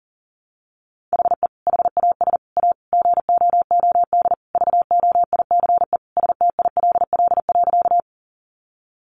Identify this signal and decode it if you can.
Morse code, heard as a keyed tone, about 40 wpm, 700 Hz. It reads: HE HAS A GOOD VOICE STILL.
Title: Sentences from Top 500 Words 40wpm